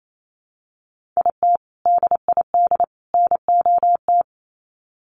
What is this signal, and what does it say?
Morse code, heard as a keyed tone, about 28 wpm, 700 Hz. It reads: IT DID NOT
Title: Sets of 3 Words 28wpm